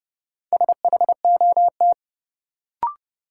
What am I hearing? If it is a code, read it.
Morse code, heard as a keyed tone, about 32 wpm, 700 Hz. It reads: SHOT
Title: Sentences from Top 1100 Words - New Word Drill Alt - 30wpm